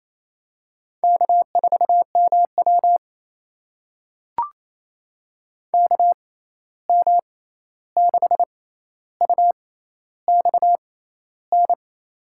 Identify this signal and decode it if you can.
Morse code, heard as a keyed tone, about 28 wpm, spaced as for 13 wpm, 700 Hz. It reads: K4MW KM6UXN
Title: USA Call Signs - 6x Character Spacing - 28wpm